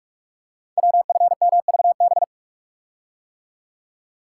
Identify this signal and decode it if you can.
Morse code, heard as a keyed tone, about 45 wpm, 690 Hz. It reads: WFMVB
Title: Code Groups - 5 Letters - 45wpm